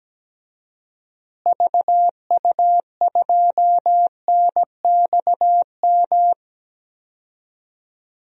Morse code, heard as a keyed tone, about 17 wpm, 700 Hz. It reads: VU2NXM
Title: DXCC-100 Call Signs - 6 Characters - Repeated 3x - 17wpm